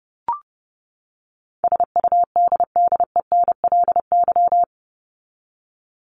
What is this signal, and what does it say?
Morse code, heard as a keyed tone, about 30 wpm, 700 Hz. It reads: SUDDENLY
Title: Top 1501-1750 Words 30wpm